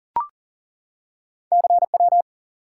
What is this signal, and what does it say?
Morse code, heard as a keyed tone, about 41 wpm, 700 Hz. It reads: CW
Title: Top 180 QSO Elements 40wpm